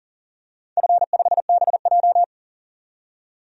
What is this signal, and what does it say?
Morse code, heard as a keyed tone, about 40 wpm, 700 Hz. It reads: F5BJ
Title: DXCC-100 Call Signs - 4 Characters - 40wpm